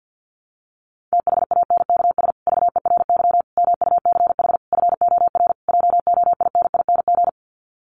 Morse code, heard as a keyed tone, about 50 wpm, 700 Hz. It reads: THANKS VERY MUCH FOR JOINING
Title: Sets of 5 Words Encore2 50wpm